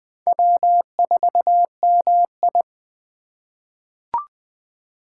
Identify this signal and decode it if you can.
Morse code, heard as a keyed tone, about 20 wpm, 700 Hz. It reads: W4MI
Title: USA Call Signs - 2x Character Spacing - 20wpm